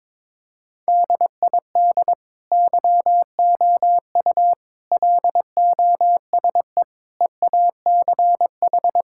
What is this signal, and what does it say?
Morse code, heard as a keyed tone, about 22 wpm, 700 Hz. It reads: DID YOU LOSE EACH
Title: Sentences from Top 1600 Words 22wpm